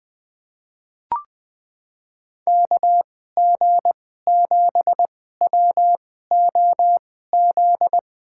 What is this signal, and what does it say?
Morse code, heard as a keyed tone, about 20 wpm, spaced as for 14 wpm, 700 Hz. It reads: KG7WOZ
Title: USA Call Signs - 2x Character Spacing - 20wpm